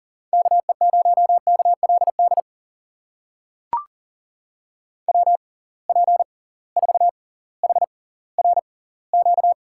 Morse code, heard as a keyed tone, about 40 wpm, spaced as for 20 wpm, 700 Hz. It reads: KE0KLD WP4HRQ
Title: USA Call Signs - 6 Characters - 6x Character Spacing - 40wpm